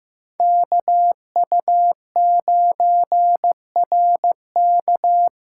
Morse code, heard as a keyed tone, about 16 wpm, 700 Hz. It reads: KU9RK
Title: K1USN SST QSO Practice - 15wpm